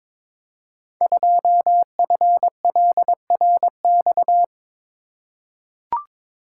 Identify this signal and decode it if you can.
Morse code, heard as a keyed tone, about 22 wpm, 700 Hz. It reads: 2FLRX
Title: Code Groups - 5 Letter:Numbers - 22wpm